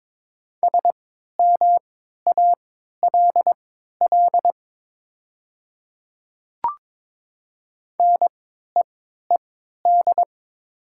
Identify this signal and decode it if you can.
Morse code, heard as a keyed tone, about 22 wpm, spaced as for 12 wpm, 700 Hz. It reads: SMALL NEED
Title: 3x Character Spacing - Top 200 Words - 22wpm